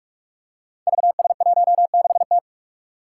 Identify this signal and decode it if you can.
Morse code, heard as a keyed tone, about 43 wpm, 700 Hz. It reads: VS16T
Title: Five Challenging Letter-Number Combinations 45wpm